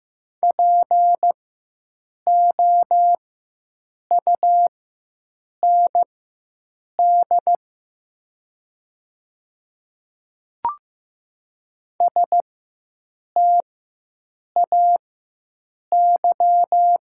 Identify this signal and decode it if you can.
Morse code, heard as a keyed tone, about 15 wpm, spaced as for 7 wpm, 700 Hz. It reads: POUND STAY
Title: 4x Character Spacing - Top 500 Words - 15wpm